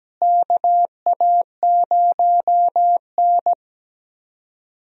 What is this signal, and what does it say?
Morse code, heard as a keyed tone, about 17 wpm, 700 Hz. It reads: KA0N
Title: USA Call Signs - 4 Characters - Repeated 3x - 17wpm